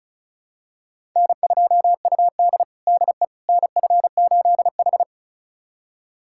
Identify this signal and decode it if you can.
Morse code, heard as a keyed tone, about 36 wpm, 700 Hz. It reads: N2UD DE NF8H
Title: USA Callsign DE Callsign - 4 Characters - 35wpm